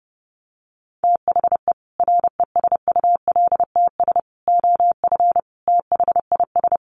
Morse code, heard as a keyed tone, about 30 wpm, 700 Hz. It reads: THE RESULTS OF THIS
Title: Sets of 5 Words 30wpm